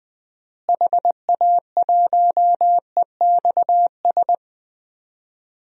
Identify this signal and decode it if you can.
Morse code, heard as a keyed tone, about 20 wpm, 700 Hz. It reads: HA1EXS
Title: DXCC-100 Call Signs - 6 Characters - Repeated 3x - 20wpm